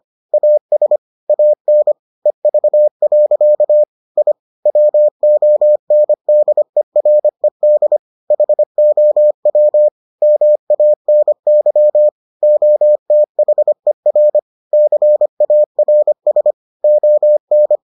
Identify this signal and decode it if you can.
Morse code, heard as a keyed tone, about 25 wpm, 600 Hz. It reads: AS AN EV. I WONDERED HOW MANY OTHER CARS ON